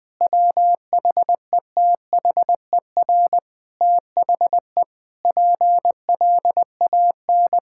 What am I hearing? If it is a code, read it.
Morse code, heard as a keyed tone, about 21 wpm, 700 Hz. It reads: WHETHER THE PLAN
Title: Sentences with 6-Word Phrases - Spoken Beginning - 20wpm